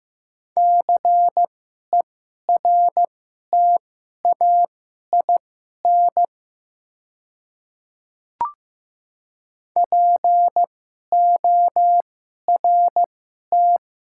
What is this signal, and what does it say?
Morse code, heard as a keyed tone, about 15 wpm, spaced as for 11 wpm, 700 Hz. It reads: CERTAIN PORT
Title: 2x Character Spacing - Top 500 Words - 15wpm